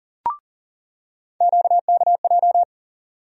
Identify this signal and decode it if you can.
Morse code, heard as a keyed tone, about 40 wpm, 700 Hz. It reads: QKJ
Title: USA Call Sign Postfixes - 3 Letters 40wpm